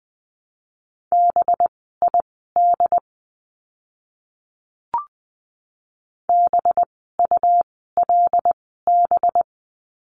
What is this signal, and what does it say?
Morse code, heard as a keyed tone, about 20 wpm, spaced as for 14 wpm, 700 Hz. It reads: BID BULB
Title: 185 Lesson 30f - Letter b - Words 20wpm